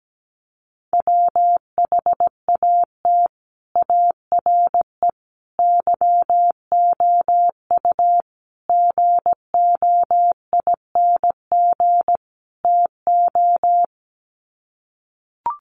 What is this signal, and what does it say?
Morse code, heard as a keyed tone, about 17 wpm, 700 Hz. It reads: WHAT ARE YOU GOING TO
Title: Sets of 5 Words 17wpm